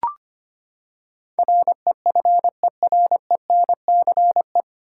Morse code, heard as a keyed tone, about 25 wpm, 700 Hz. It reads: REFERENCE
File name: Nine Letter Words - 25wpm